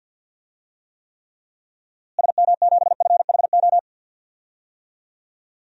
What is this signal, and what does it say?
Morse code, heard as a keyed tone, about 49 wpm, 700 Hz. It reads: SM7FHO
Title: DXCC-100 Call Signs - 6 Characters - Repeated 3x - 50wpm